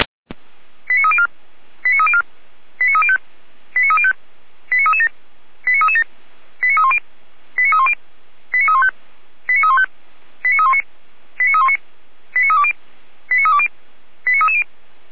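Five-tone selective call (ZVEI), heard as quick runs of five tones, tones of about 70 ms. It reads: 89304 89304 89305 89305 89337 89337 89310 89310 89315 89315 89319 89319 89320 89320 89330
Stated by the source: ZVEI five-tone call 89303 received by SDR.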